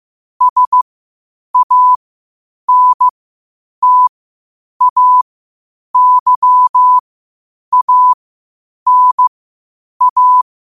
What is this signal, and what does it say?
Morse code, heard as a keyed tone, about 15 wpm, spaced as for 8 wpm, 1000 Hz. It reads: SANTAYANA
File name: qotdfiles / 8wpm